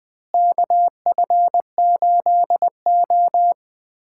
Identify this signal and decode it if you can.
Morse code, heard as a keyed tone, about 21 wpm, 700 Hz. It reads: KF8O
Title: K1USN SST QSO Practice - 20wpm